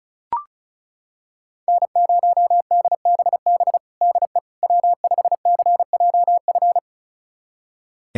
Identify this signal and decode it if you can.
Morse code, heard as a keyed tone, about 36 wpm, 700 Hz. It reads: N0DBB DE W5CJF
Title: USA Callsign DE Callsign - 5 Characters - 35wpm